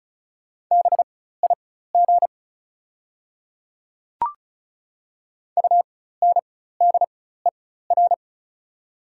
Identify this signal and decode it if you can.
Morse code, heard as a keyed tone, about 36 wpm, spaced as for 16 wpm, 690 Hz. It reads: BIG UNDER
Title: 4x Character Spacing - Top 500 Words - 35wpm